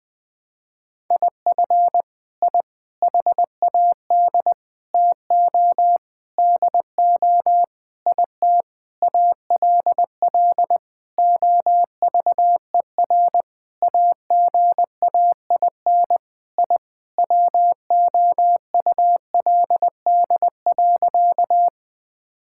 Morse code, heard as a keyed tone, about 19 wpm, 700 Hz. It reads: IF I HAD TO DO IT ALL OVER AGAIN I WOULD.
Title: Sentences with 9-Word Phrases - 20wpm